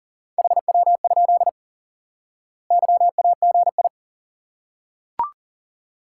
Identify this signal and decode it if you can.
Morse code, heard as a keyed tone, about 40 wpm, 700 Hz. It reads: HW? YAGI
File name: IQE Course 3 - Lesson 89 - YAGI - Two Elements - 40wpm